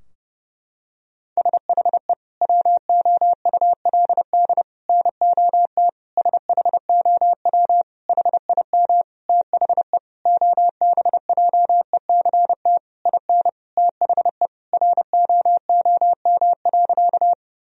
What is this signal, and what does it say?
Morse code, heard as a keyed tone, about 30 wpm, 700 Hz. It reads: SHE WOULD NOT SHOW HIM THE OBJECT IN THE ROOM.
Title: Sentences from Top 1000 Words - Review Part 3 - 30wpm